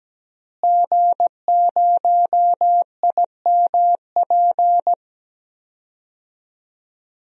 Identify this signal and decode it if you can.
Morse code, heard as a keyed tone, about 17 wpm, 700 Hz. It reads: G0IMP
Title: DXCC-100 Call Signs - 5 Characters - Repeated 2x - 17wpm